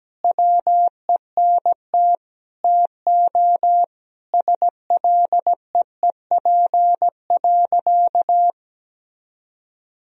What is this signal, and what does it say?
Morse code, heard as a keyed tone, about 17 wpm, 700 Hz. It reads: WENT TO SLEEP.
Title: Sentences from Top 1000 Words - Review Part 2 - 17wpm